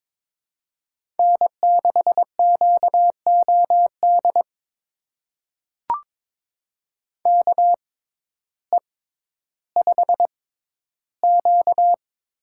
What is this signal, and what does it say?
Morse code, heard as a keyed tone, about 22 wpm, spaced as for 12 wpm, 700 Hz. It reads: N6QOD KE5Q
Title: USA Call Signs - 6x Character Spacing - 22wpm